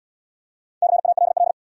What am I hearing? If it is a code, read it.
Morse code, heard as a keyed tone, about 75 wpm, 700 Hz. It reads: CALL